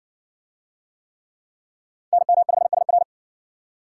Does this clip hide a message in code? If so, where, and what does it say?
Yes: Morse code from 2.1 to 3.0 s, heard as a keyed tone, about 59 wpm, 700 Hz. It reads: NM5SR